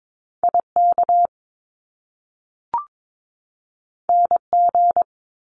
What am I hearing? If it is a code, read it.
Morse code, heard as a keyed tone, about 22 wpm, 700 Hz. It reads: IK NG
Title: Top Two Letter Combinations in Words 22wpm